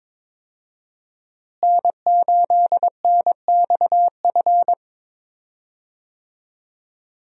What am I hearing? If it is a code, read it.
Morse code, heard as a keyed tone, about 21 wpm, 700 Hz. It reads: N8NXF
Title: USA Call Signs - 5 Characters - Repeated 2x - 22wpm